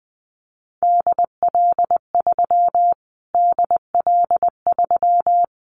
Morse code, heard as a keyed tone, about 20 wpm, 700 Hz. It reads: DL3 DL3
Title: Callsign Prefixes for Top 25 DXCC - Repeated 3x - 20wpm